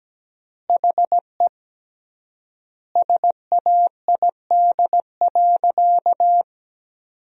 Morse code, heard as a keyed tone, about 17 wpm, 700 Hz. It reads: HE SAID.
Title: 3x Word-Spacing - Sentences from Top 100 Words - 17wpm